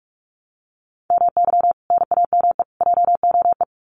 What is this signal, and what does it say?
Morse code, heard as a keyed tone, about 44 wpm, 700 Hz. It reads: MY NAME JOE
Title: Common QSO Phrases 45wpm